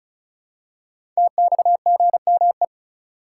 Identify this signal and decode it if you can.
Morse code, heard as a keyed tone, about 34 wpm, 700 Hz. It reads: TXGME
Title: Code Groups - 5 Letter:Numbers - 35wpm